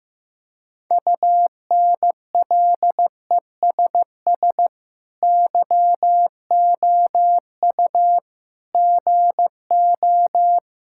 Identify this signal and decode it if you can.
Morse code, heard as a keyed tone, about 15 wpm, 700 Hz. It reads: UNLESS YOU GO
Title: Sentences from Top 1000 Words - Review Part 6 - 15wpm